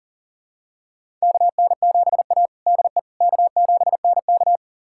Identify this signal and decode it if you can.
Morse code, heard as a keyed tone, about 40 wpm, 700 Hz. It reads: KN7A DE K7NK
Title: USA Callsign DE Callsign - 4 Characters - 40wpm